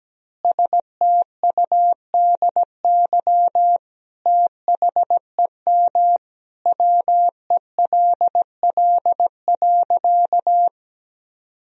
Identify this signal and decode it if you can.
Morse code, heard as a keyed tone, about 17 wpm, 700 Hz. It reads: STUDY THEM WELL.
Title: Sentences from Top 300 Words and Abbreviations 17wpm